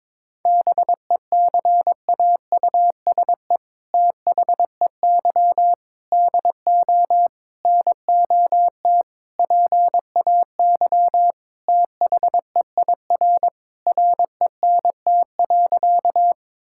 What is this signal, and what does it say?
Morse code, heard as a keyed tone, about 22 wpm, 700 Hz. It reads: BECAUSE THEY DO NOT PAY THEIR RENT.
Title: Sentences from Top 1000 Words - Review Part 5 - 22wpm